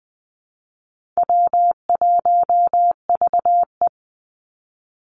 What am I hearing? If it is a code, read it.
Morse code, heard as a keyed tone, about 20 wpm, 700 Hz. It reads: W1VE
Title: Easy Call Signs - 20wpm